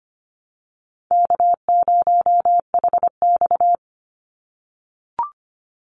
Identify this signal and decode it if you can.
Morse code, heard as a keyed tone, about 25 wpm, 700 Hz. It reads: K0HX
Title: Easy Call Signs - 25wpm